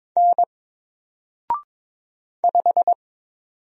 Morse code, heard as a keyed tone, about 22 wpm, 700 Hz. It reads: N 5
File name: Single-Letter-Number-22wpm